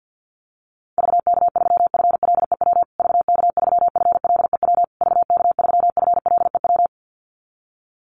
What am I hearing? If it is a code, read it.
Morse code, heard as a keyed tone, about 50 wpm, 700 Hz. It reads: VK3FLEW VK3FLEW VK3FLEW
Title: USA Call Signs Repeated 3x 50wpm